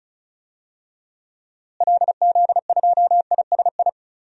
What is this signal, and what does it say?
Morse code, heard as a keyed tone, about 36 wpm, 700 Hz. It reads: LZ2ISI
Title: DXCC-100 Call Signs - 6 Characters - Repeated 3x - 35wpm